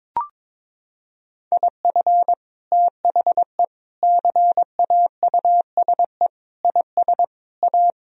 Morse code, heard as a keyed tone, about 22 wpm, 700 Hz. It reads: IF THE CAUSE IS A
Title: Sentences from Top 1000 Words - Review Part 1 - 22wpm